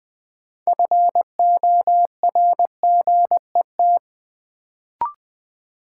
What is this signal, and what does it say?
Morse code, heard as a keyed tone, about 20 wpm, 700 Hz. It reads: FORGET
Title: Sentences from Top 1200 Words - New Word Drill - 20wpm